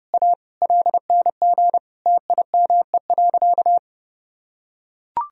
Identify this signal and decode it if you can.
Morse code, heard as a keyed tone, about 32 wpm, 700 Hz. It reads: A LNG TIME.
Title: Sentences from Top 400 Words and Abbreviations 30wpm